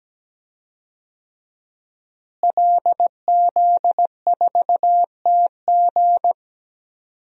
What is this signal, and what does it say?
Morse code, heard as a keyed tone, about 17 wpm, 700 Hz. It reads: LZ4TG
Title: DXCC-100 Call Signs - 5 Characters - Repeated 2x - 17wpm